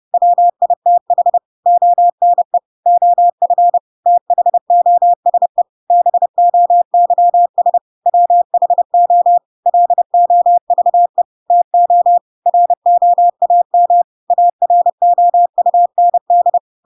Morse code, heard as a keyed tone, about 30 wpm, 700 Hz. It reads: WITH ONE OF THOSE BOYS WHO LOVE TO ROAM AROUND
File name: Pinocchio / Pinocchio ch6 30 30 ik1hge